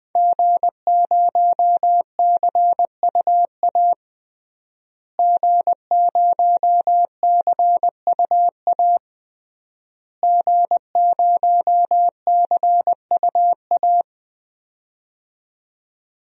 Morse code, heard as a keyed tone, about 21 wpm, 700 Hz. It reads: G0CUA G0CUA G0CUA
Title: DXCC-100 Call Signs - 5 Characters - Repeated 3x - 20wpm